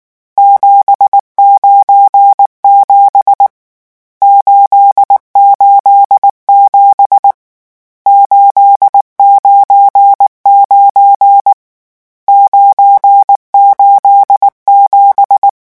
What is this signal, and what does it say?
Morse code, heard as a keyed tone, about 19 wpm, 800 Hz. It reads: 797 887 899 987
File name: morse mp3 / lecon12-vitesse 18